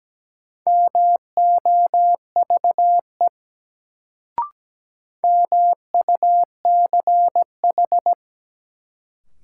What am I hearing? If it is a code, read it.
Morse code, heard as a keyed tone, about 17 wpm, 700 Hz. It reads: MOVE MUCH
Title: Sentences from Top 200 Words - New Word Drill - 17wpm